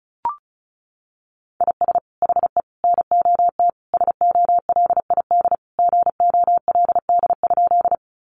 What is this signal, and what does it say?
Morse code, heard as a keyed tone, about 34 wpm, 700 Hz. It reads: IS HE NOT SOLID GOLD?
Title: Sentences from Top 1500 Words 35wpm